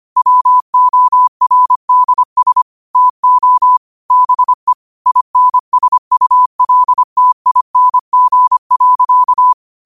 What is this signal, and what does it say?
Morse code, heard as a keyed tone, about 25 wpm, 1000 Hz. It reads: WORDS TO BE INSULTING.
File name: qotdfiles / 25wpm